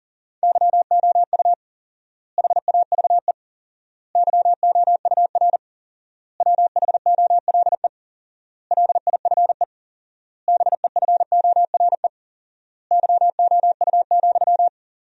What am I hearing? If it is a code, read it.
Morse code, heard as a keyed tone, about 40 wpm, 700 Hz. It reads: YOU HAVE YOUR WHOLE LIFE BEFORE YOU,
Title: 4x Word-Spacing - Sentences from Top 400 Words - 40wpm